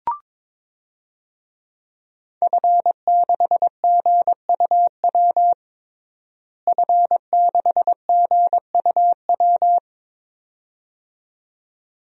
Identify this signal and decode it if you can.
Morse code, heard as a keyed tone, about 22 wpm, 700 Hz. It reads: F6GUW F6GUW
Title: DXCC-100 Call Signs - 5 Characters - Repeated 2x - 22wpm